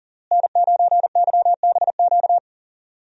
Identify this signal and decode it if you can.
Morse code, heard as a keyed tone, about 40 wpm, 700 Hz. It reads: N9YBQ
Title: USA Call Signs - 5 Characters - Repeated 3x - 40wpm